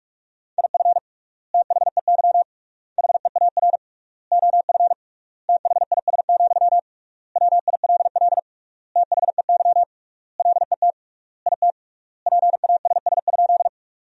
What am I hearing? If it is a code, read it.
Morse code, heard as a keyed tone, about 44 wpm, 700 Hz. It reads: IF THEY HEAR OF THIS, WILL THEY LET IT PASS?
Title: 3x Word-Spacing - Sentences from Top 400 Words - 45wpm